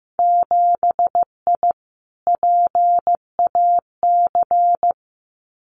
Morse code, heard as a keyed tone, about 15 wpm, 700 Hz. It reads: 7I PAC
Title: Winter Field Day Exchanges Encore 15wpm